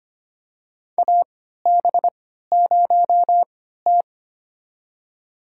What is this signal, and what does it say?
Morse code, heard as a keyed tone, about 25 wpm, spaced as for 14 wpm, 700 Hz. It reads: AB0T
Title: USA Call Signs - 3x Character Spacing - 25wpm